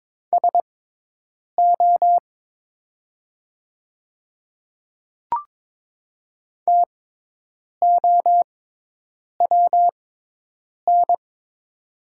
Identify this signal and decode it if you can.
Morse code, heard as a keyed tone, about 22 wpm, spaced as for 8 wpm, 700 Hz. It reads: SO TOWN
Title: 6x Character Spacing - Top 500 Words - 22wpm